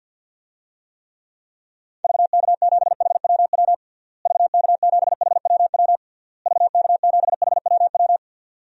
Morse code, heard as a keyed tone, about 52 wpm, 700 Hz. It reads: VK7HWW VK7HWW VK7HWW
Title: DXCC-100 Call Signs - 6 Characters - Repeated 3x - 50wpm